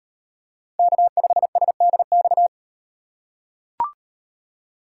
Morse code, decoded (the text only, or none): K5SDX